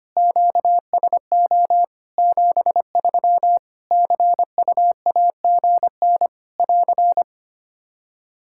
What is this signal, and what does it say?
Morse code, heard as a keyed tone, about 25 wpm, 700 Hz. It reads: QSO 73 CUAGN <AR>
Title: K1USN SST QSO Practice 25wpm